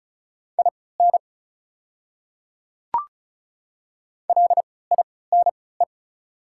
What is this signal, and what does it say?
Morse code, heard as a keyed tone, about 36 wpm, spaced as for 20 wpm, 700 Hz. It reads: IN LINE
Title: 3x Character Spacing - Top 200 Words - 35wpm